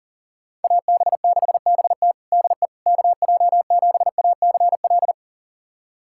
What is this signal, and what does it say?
Morse code, heard as a keyed tone, about 40 wpm, 700 Hz. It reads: AB6BT DE KJ7ACL
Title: USA Callsign DE Callsign 40wpm